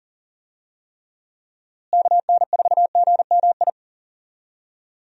Morse code, heard as a keyed tone, about 40 wpm, 700 Hz. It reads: KN4GMI
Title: USA Call Signs - 6 Characters - Repeated 3x - 40wpm